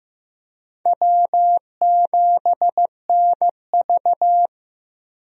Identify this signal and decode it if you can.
Morse code, heard as a keyed tone, about 15 wpm, 700 Hz. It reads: W7NV
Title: USA Call Signs - 4 Characters - 15wpm